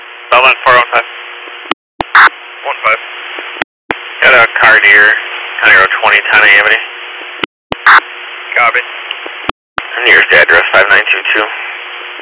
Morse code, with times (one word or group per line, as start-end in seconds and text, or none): none